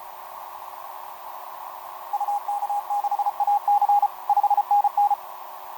Morse code, heard as a keyed tone, about 32 wpm, 820 Hz. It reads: UK6AC 5NN
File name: jt5dx f5idj etc 12m cw 18mar2011 0915